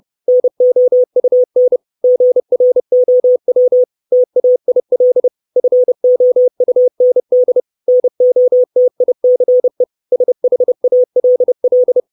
Morse code, heard as a keyed tone, about 30 wpm, 500 Hz. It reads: NOUN GROW TAIL FOUND NOTICE SHALL
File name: 632 Words 30WPM